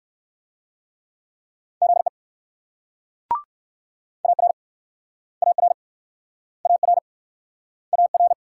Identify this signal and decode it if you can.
Morse code, heard as a keyed tone, about 64 wpm, 700 Hz. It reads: ZE AR AR AR AR